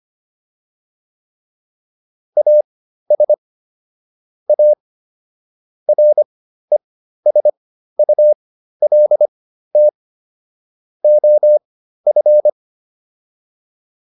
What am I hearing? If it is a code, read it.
Morse code, heard as a keyed tone, about 25 wpm, spaced as for 13 wpm, 600 Hz. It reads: AS A RESULT OF
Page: /intermediate-practice-files/